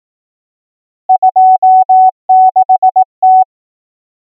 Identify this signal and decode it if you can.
Morse code, heard as a keyed tone, about 18 wpm, 750 Hz. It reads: 26T
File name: numbers / V2-morse